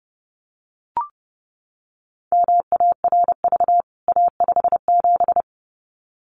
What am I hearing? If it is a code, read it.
Morse code, heard as a keyed tone, about 30 wpm, 700 Hz. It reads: MARV A57
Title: CWT Exchange 30wpm